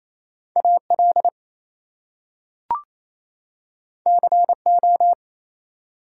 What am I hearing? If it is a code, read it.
Morse code, heard as a keyed tone, about 28 wpm, 700 Hz. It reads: AL CO
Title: US States and Canadian Provinces 28wpm